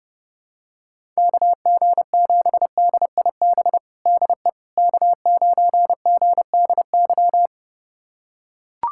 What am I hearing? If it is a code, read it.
Morse code, heard as a keyed tone, about 30 wpm, 700 Hz. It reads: KG7DIB DE K9GDY
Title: USA Callsign DE Callsign 30wpm